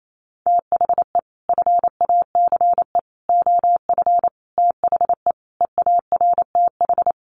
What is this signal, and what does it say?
Morse code, heard as a keyed tone, about 28 wpm, 700 Hz. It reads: THE FACE OF THE EARTH
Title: Sets of 5 Words Encore 28wpm